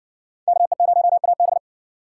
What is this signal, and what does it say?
Morse code, heard as a keyed tone, about 60 wpm, 700 Hz. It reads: KE0AB